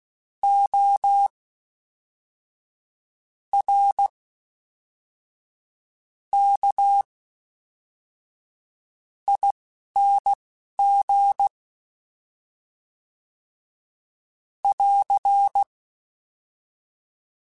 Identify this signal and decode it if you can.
Morse code, heard as a keyed tone, about 16 wpm, spaced as for 5 wpm, 800 Hz. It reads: ORKING <AR>
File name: Special INGs / Special INGs-mp3s / Fast ing @10